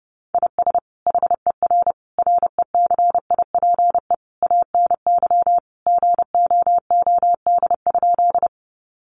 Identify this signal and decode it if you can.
Morse code, heard as a keyed tone, about 30 wpm, 700 Hz. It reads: IS HER RECIPE ANY GOOD?
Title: Sentences from Top 1200 Words 30wpm